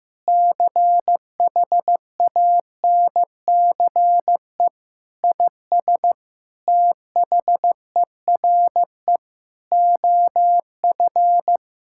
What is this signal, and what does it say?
Morse code, heard as a keyed tone, about 15 wpm, 700 Hz. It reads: CHANCE IS THERE OF
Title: Sentences from Top 1000 Words - Review Part 6 - 15wpm